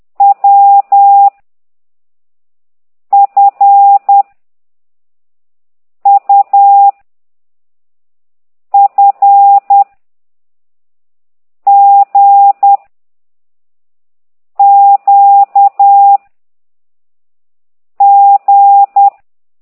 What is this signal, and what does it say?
Morse code, heard as a keyed tone, about 10 wpm, 800 Hz. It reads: W F U F G Q G
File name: ct2idw / exerc11